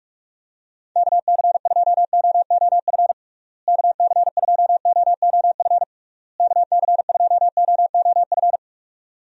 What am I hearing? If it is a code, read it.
Morse code, heard as a keyed tone, about 46 wpm, 700 Hz. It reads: KC2OOF KC2OOF KC2OOF
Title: USA Call Signs - 6 Characters - Repeated 3x - 45wpm